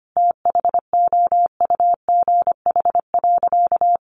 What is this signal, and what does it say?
Morse code, heard as a keyed tone, about 25 wpm, 700 Hz. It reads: THOUGH.
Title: Sentences from Top 1500 Words 25wpm